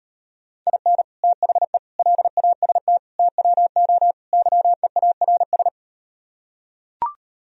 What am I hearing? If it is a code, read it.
Morse code, heard as a keyed tone, about 38 wpm, 700 Hz. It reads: IN THE LAST TWO YEARS